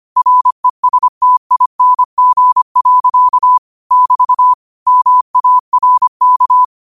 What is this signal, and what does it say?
Morse code, heard as a keyed tone, about 25 wpm, 1000 Hz. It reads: RESTING. <BT> MARK
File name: qotdfiles / 25wpm